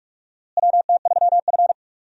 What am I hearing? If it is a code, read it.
Morse code, heard as a keyed tone, about 43 wpm, 700 Hz. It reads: WT3F
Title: USA Call Signs - 4 Characters - Repeated 3x - 45wpm